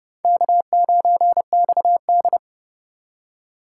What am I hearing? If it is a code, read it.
Morse code, heard as a keyed tone, about 31 wpm, 700 Hz. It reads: K9XD
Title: K1USN SST QSO Practice - 30wpm